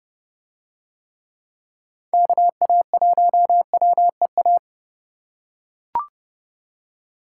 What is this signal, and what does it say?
Morse code, heard as a keyed tone, about 29 wpm, 700 Hz. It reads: KA1WEA